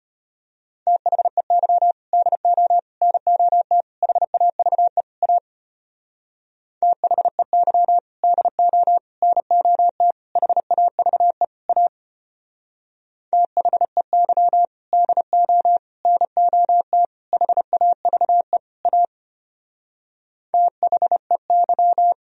Morse code, heard as a keyed tone, about 32 wpm, 700 Hz. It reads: THEY DO NOT HAVE A THEY DO NOT HAVE A THEY DO NOT HAVE A THEY